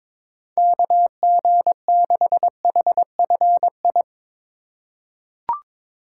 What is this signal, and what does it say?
Morse code, heard as a keyed tone, about 22 wpm, 700 Hz. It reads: KG6HFI
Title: USA Call Signs - 3x Character Spacing - 22wpm